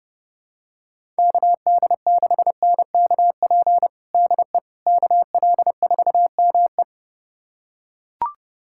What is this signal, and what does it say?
Morse code, heard as a keyed tone, about 32 wpm, 700 Hz. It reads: KD6NKP DE KL4ME
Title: USA Callsign DE Callsign 30wpm